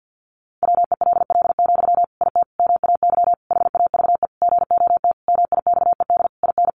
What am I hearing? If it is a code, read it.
Morse code, heard as a keyed tone, about 50 wpm, 700 Hz. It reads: WELL, IT MAY HAVE GOT MIXED IN
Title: Sentences from Top 1500 Words 50wpm